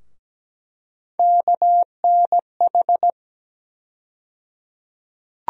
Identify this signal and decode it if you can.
Morse code, heard as a keyed tone, about 17 wpm, 700 Hz. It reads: KNH
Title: USA Call Sign Postfixes - 3 Letters 17wpm